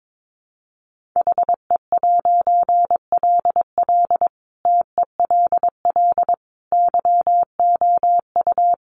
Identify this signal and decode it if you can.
Morse code, heard as a keyed tone, about 22 wpm, 700 Hz. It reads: HE'LL TELL YOU
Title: A Sound Of Thunder 22wpm